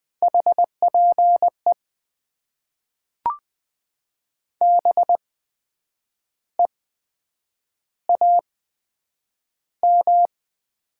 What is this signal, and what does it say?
Morse code, heard as a keyed tone, about 20 wpm, spaced as for 8 wpm, 700 Hz. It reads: HPE BEAM
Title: Top 90 QSO Elements - 8x Character Spacing - 20wpm